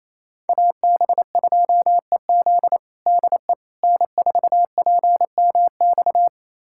Morse code, heard as a keyed tone, about 28 wpm, 700 Hz. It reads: AB2EZ DE N4PMX
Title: USA Callsign DE Callsign - 5 Characters - 28wpm